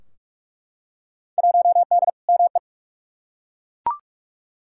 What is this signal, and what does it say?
Morse code, heard as a keyed tone, about 42 wpm, 700 Hz. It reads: 1D ME
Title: Field Day Exchanges - 45wpm